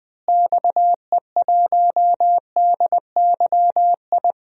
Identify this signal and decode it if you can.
Morse code, heard as a keyed tone, about 20 wpm, 700 Hz. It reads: XE1DYI
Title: DXCC-100 Call Signs - 6 Characters - Repeated 3x - 20wpm